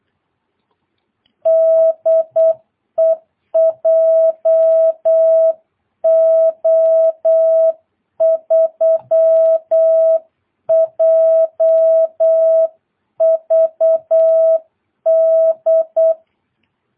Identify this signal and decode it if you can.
Morse code, heard as a keyed tone, about 8 wpm, 660 Hz. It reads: DE JO3JVD
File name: jh8cbh / 2022cw